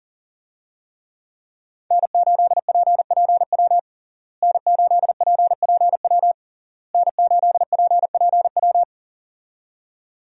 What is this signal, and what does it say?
Morse code, heard as a keyed tone, about 40 wpm, 700 Hz. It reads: N8PPW N8PPW N8PPW
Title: USA Call Signs - 5 Characters - Repeated 3x - 40wpm